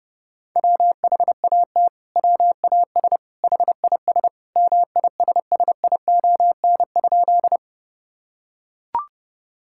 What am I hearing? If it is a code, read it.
Morse code, heard as a keyed tone, about 32 wpm, 700 Hz. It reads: WHAT WAS HIS MISSION?
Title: Sentences from Top 1600 Words 30wpm